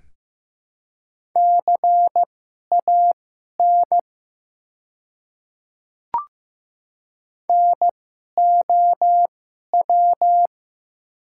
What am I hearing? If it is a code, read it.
Morse code, heard as a keyed tone, about 15 wpm, spaced as for 11 wpm, 690 Hz. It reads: CAN NOW
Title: 2x Character Spacing - Top 100 Words - 15wpm